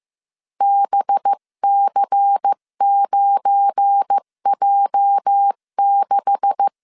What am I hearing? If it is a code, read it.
Morse code, heard as a keyed tone, about 15 wpm, 800 Hz. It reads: BC9J6
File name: CW / DET / 1 9 1500C S B 14wpm 3302